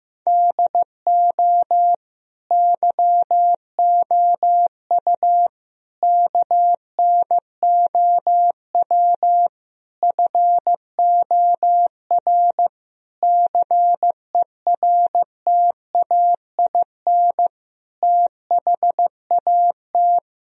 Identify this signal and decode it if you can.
Morse code, heard as a keyed tone, about 15 wpm, 700 Hz. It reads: DO YOU KNOW FOR CERTAIN THAT
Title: Sentences from Top 1000 Words - Review Part 2 - 15wpm